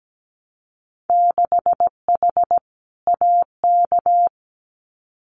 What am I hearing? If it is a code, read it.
Morse code, heard as a keyed tone, about 17 wpm, 700 Hz. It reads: 6H AK
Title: Winter Field Day Exchanges Encore 17wpm